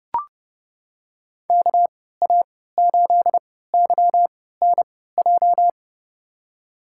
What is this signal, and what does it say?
Morse code, heard as a keyed tone, about 30 wpm, spaced as for 17 wpm, 700 Hz. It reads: KA8YNJ
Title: USA Call Signs - 3x Character Spacing - 30wpm